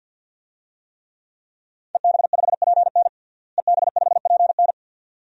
Morse code, heard as a keyed tone, about 52 wpm, 700 Hz. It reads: EB5PN EB5PN
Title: DXCC-100 Call Signs - 5 Characters - Repeated 2x - 50wpm